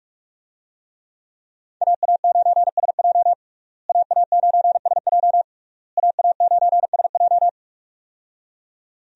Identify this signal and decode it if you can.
Morse code, heard as a keyed tone, about 46 wpm, 700 Hz. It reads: AA9SJ AA9SJ AA9SJ
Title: USA Call Signs - 5 Characters - Repeated 3x - 45wpm